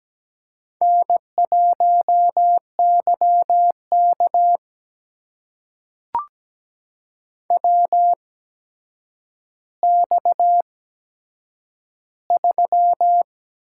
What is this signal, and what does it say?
Morse code, heard as a keyed tone, about 17 wpm, spaced as for 8 wpm, 700 Hz. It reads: N1YK WX3
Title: USA Call Signs - 4 Characters - 8x Character Spacing - 17wpm